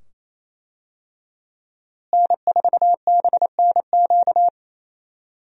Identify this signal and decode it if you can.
Morse code, heard as a keyed tone, about 27 wpm, 700 Hz. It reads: N4BNQ
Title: USA Call Signs - 5 Characters - Repeated 2x - 28wpm